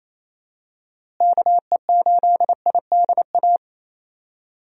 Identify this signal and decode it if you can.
Morse code, heard as a keyed tone, about 28 wpm, 700 Hz. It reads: KE8IDA
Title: USA Call Signs - 6 Characters - Repeated 3x - 28wpm